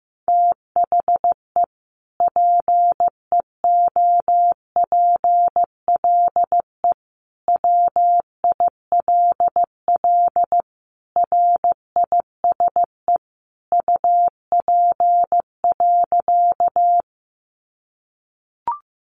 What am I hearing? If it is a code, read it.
Morse code, heard as a keyed tone, about 15 wpm, 700 Hz. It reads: THE PEOPLE WILL RISE UP.
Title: Sentences from Top 900 Words 15wpm